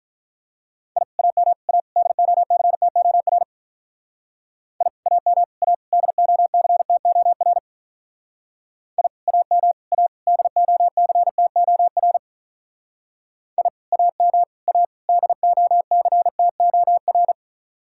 Morse code, heard as a keyed tone, about 45 wpm, 700 Hz. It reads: I AM A DOCTOR I AM A DOCTOR I AM A DOCTOR I AM A DOCTOR